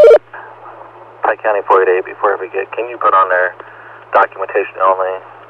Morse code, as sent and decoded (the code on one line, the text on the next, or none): none